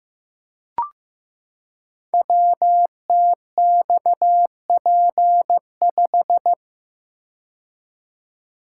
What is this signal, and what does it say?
Morse code, heard as a keyed tone, about 15 wpm, 700 Hz. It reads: WTXP5
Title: Code Groups - 5 Letter:Numbers - 15wpm